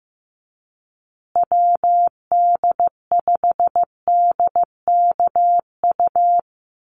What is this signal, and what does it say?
Morse code, heard as a keyed tone, about 14 wpm, 700 Hz. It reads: WD5DKU
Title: USA Call Signs Repeated 3x 15wpm